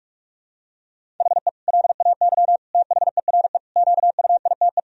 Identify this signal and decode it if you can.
Morse code, heard as a keyed tone, about 46 wpm, 700 Hz. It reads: HE LAY THERE QUITE
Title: Sentences from Top 1000 Words - Review Part 2 - 45wpm